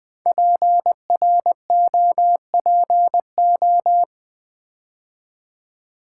Morse code, heard as a keyed tone, about 20 wpm, 700 Hz. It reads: PROPO
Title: Build A Word - 5 Letters At A Time - 10-Letter Words - 20wpm